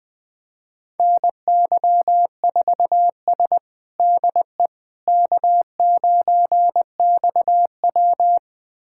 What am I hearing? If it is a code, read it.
Morse code, heard as a keyed tone, about 20 wpm, 700 Hz. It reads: NY4S DE K9XW
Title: USA Callsign DE Callsign - 4 Characters - 20wpm